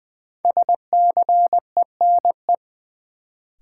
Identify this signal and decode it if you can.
Morse code, heard as a keyed tone, about 20 wpm, 700 Hz. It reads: SCENE
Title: Sentences from Top 1100 Words - New Word Drill - 20wpm